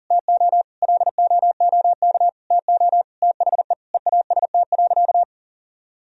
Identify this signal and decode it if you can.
Morse code, heard as a keyed tone, about 40 wpm, 700 Hz. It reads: TO LOOK TO THE EAST.
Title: Sentences from Top 1000 Words - Review Part 3 - 40wpm